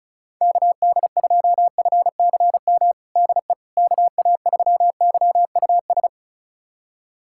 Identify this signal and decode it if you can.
Morse code, heard as a keyed tone, about 36 wpm, 700 Hz. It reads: KD2FCM DE KA3YUS
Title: USA Callsign DE Callsign - 6 Characters - 35wpm